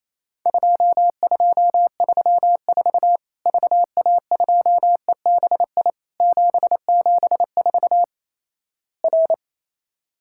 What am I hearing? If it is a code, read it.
Morse code, heard as a keyed tone, about 28 wpm, 700 Hz. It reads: 2234 VA2EBI 774 R
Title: Worked All Europe DX Contest - QTC Exchange - 28wpm